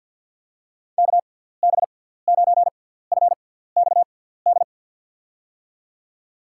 Morse code, heard as a keyed tone, about 50 wpm, spaced as for 17 wpm, 700 Hz. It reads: KB9FXD
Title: USA Call Signs - 6 Characters - 6x Character Spacing - 50wpm